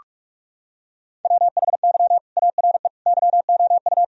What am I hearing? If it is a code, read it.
Morse code, heard as a keyed tone, about 42 wpm, 700 Hz. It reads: WHY ARE YOU
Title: Sentences from Top 1000 Words - Review Part 4 - 45wpm